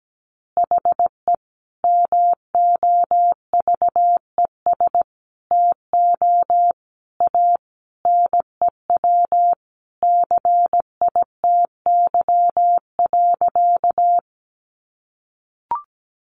Morse code, heard as a keyed tone, about 17 wpm, 700 Hz. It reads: HE MOVES TO A NEW CITY.
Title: Easy Phrases 17wpm